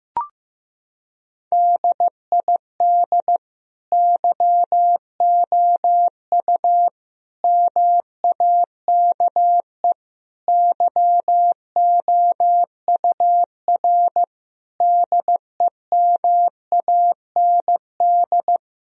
Morse code, heard as a keyed tone, about 16 wpm, 700 Hz. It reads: DID YOU MAKE YOUR DEMAND
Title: Sentences from Top 1000 Words - Review Part 4 - 15wpm